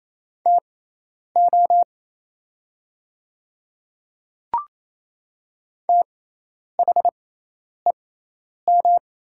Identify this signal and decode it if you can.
Morse code, heard as a keyed tone, about 28 wpm, spaced as for 10 wpm, 700 Hz. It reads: TO THEM
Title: 6x Character Spacing - Top 100 Words - 28wpm